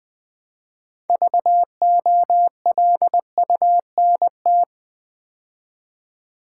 Morse code, heard as a keyed tone, about 20 wpm, 700 Hz. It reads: VOLUNT
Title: Build A Word - 3 Letters At A Time - 9-Letter Words - 20wpm